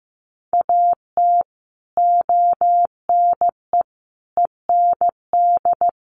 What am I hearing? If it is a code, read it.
Morse code, heard as a keyed tone, about 15 wpm, 700 Hz. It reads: AT ONE END
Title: Sets of 5 Words Encore 15wpm